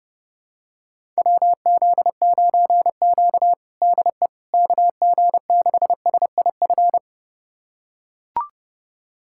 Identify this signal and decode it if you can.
Morse code, heard as a keyed tone, about 32 wpm, 700 Hz. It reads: WZ9Q DE KG6SIF
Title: USA Callsign DE Callsign 30wpm